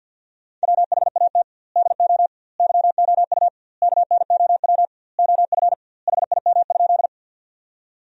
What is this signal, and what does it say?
Morse code, heard as a keyed tone, about 50 wpm, 700 Hz. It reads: WHAT DO YOU KNOW OF HIM?
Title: 2x Word-Spacing - Sentences from Top 100 Words - 50wpm